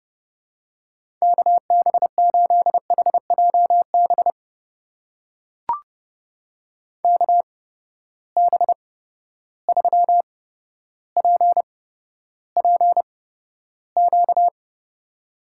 Morse code, heard as a keyed tone, about 30 wpm, spaced as for 13 wpm, 700 Hz. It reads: KB8HJB KB3PPQ
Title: USA Call Signs - 6 Characters - 8x Character Spacing - 30wpm